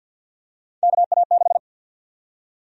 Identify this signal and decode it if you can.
Morse code, heard as a keyed tone, about 53 wpm, 700 Hz. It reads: KA6
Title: USA Call Sign Prefixes - Repeated 2x - 50wpm